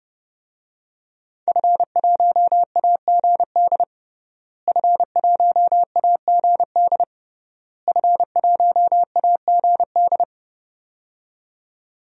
Morse code, heard as a keyed tone, about 30 wpm, 700 Hz. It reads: F1AGD F1AGD F1AGD
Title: DXCC-100 Call Signs - 5 Characters - Repeated 3x - 30wpm